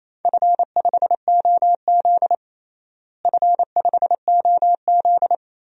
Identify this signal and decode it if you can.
Morse code, heard as a keyed tone, about 28 wpm, 700 Hz. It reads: F5OZ F5OZ
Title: DXCC-100 Call Signs - 4 Characters - Repeated 2x - 28wpm